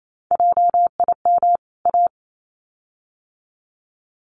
Morse code, heard as a keyed tone, about 28 wpm, 700 Hz. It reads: JIM A
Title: CWT Exchange 28wpm